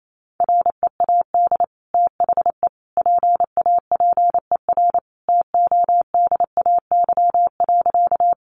Morse code, heard as a keyed tone, about 28 wpm, 700 Hz. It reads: READ THE PAPER TODAY.
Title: Sentences from Top 600 Words 28wpm